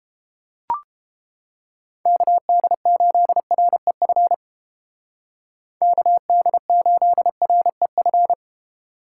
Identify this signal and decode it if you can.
Morse code, heard as a keyed tone, about 30 wpm, 700 Hz. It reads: KD8REF KD8REF